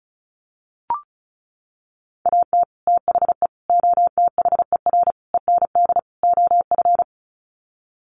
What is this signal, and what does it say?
Morse code, heard as a keyed tone, about 34 wpm, 700 Hz. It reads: AT THE OTHER END OF
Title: Sets of 5 Words 35wpm